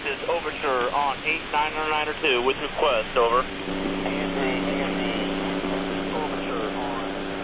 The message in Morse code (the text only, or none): none